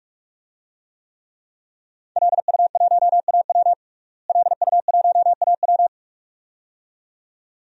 Morse code, heard as a keyed tone, about 44 wpm, 700 Hz. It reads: LU1AW LU1AW
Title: DXCC-100 Call Signs - 5 Characters - Repeated 2x - 45wpm